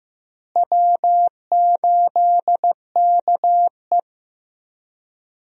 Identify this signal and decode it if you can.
Morse code, heard as a keyed tone, about 15 wpm, 700 Hz. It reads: W8KE
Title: USA Call Signs - 4 Characters - Repeated 3x - 15wpm